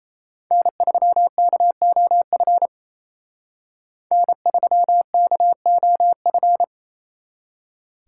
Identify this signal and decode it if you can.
Morse code, heard as a keyed tone, about 35 wpm, 700 Hz. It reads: N3KOF N3KOF